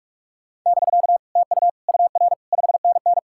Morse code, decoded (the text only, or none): <BK> TU UR 5NN